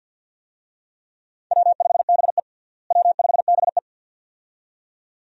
Morse code, heard as a keyed tone, about 46 wpm, 700 Hz. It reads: W5BE W5BE
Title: USA Call Signs - 4 Characters - Repeated 2x - 50wpm